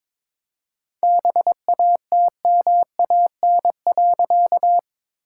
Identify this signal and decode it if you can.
Morse code, heard as a keyed tone, about 22 wpm, 700 Hz. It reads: BATMAN.
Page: morse-code-ninja-practice-22wpm